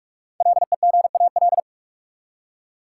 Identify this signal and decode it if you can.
Morse code, heard as a keyed tone, about 44 wpm, 700 Hz. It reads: LEGAL
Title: Five Letter Words - 45wpm